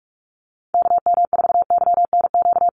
Morse code, heard as a keyed tone, about 44 wpm, 700 Hz. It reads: KM4YNQ
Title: USA Call Signs Repeated 2x 45wpm